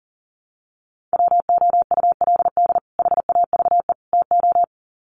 Morse code, heard as a keyed tone, about 40 wpm, 700 Hz. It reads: WOULD HAVE TO
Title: Sets of 3 Words 40wpm